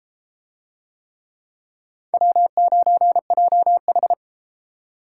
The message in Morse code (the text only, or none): W9JH